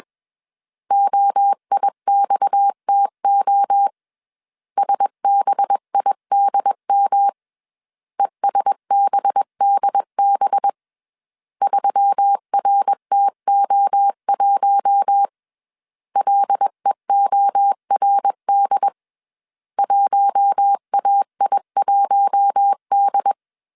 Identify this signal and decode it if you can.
Morse code, heard as a keyed tone, about 21 wpm, 800 Hz. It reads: OIXTO SBIDM ESBDB 3RTO1 LEORD 1AI1D
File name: CW / DET / 1 6 1500C S B 20wpm 2048